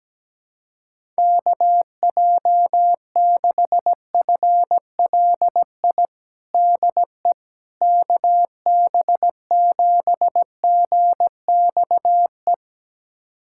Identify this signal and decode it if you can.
Morse code, heard as a keyed tone, about 17 wpm, 700 Hz. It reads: KJ6FLI DE KB7GXE
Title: USA Callsign DE Callsign - 6 Characters - 17wpm